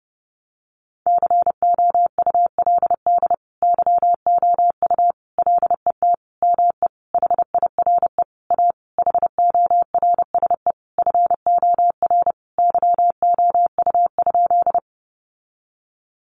Morse code, heard as a keyed tone, about 30 wpm, 700 Hz. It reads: COULD YOU LET ME HIRE A HORSE FOR YOU?
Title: Sentences from Top 1400 Words 30wpm